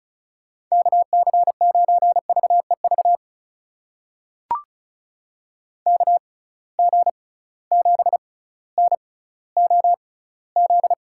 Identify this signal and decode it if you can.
Morse code, heard as a keyed tone, about 36 wpm, spaced as for 18 wpm, 700 Hz. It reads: KC9VEV KG7NOZ
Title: USA Call Signs - 6 Characters - 6x Character Spacing - 35wpm